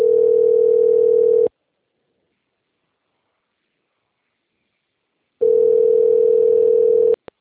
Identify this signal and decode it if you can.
Ringback tone (North America), 440 + 480 Hz.
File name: out- - - - - .37290